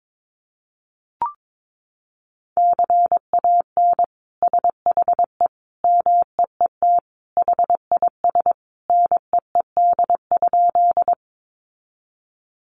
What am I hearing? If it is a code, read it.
Morse code, heard as a keyed tone, about 22 wpm, 700 Hz. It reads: CAN SHE MEET HIS NEED?
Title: Sentences from Top 1100 Words 22wpm